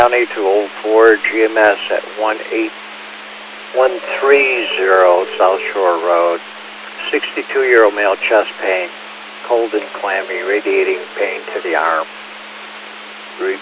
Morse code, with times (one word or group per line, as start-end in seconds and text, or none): none